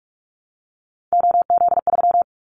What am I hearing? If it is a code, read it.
Morse code, heard as a keyed tone, about 44 wpm, 700 Hz. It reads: OZ3
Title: Callsign Prefixes for Top 100 DXCC - 45wpm